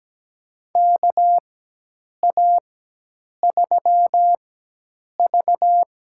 Morse code, heard as a keyed tone, about 17 wpm, spaced as for 8 wpm, 700 Hz. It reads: KA3V
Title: USA Call Signs - 4x Character Spacing - 17wpm